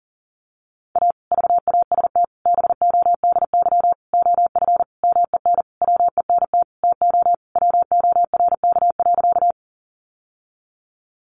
Morse code, heard as a keyed tone, about 37 wpm, 700 Hz. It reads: A VAST BODY OF MEN WENT TO WORK.
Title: Sentences from Top 1300 Words 40wpm